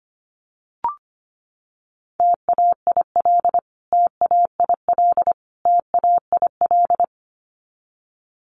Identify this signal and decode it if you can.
Morse code, heard as a keyed tone, about 25 wpm, 700 Hz. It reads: TAIL TAIL TAIL
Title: Top 500 Words Repeated 25wpm